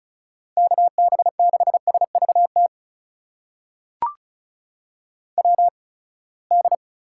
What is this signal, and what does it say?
Morse code, heard as a keyed tone, about 34 wpm, spaced as for 24 wpm, 700 Hz. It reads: KB6SVT WD
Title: USA Call Signs - 6 Characters - 8x Character Spacing - 35wpm